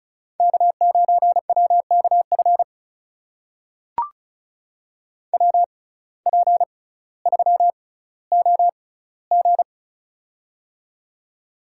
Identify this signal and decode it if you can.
Morse code, heard as a keyed tone, about 36 wpm, spaced as for 18 wpm, 700 Hz. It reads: K9WKF WP3OG
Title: USA Call Signs - 5 Characters - 6x Character Spacing - 35wpm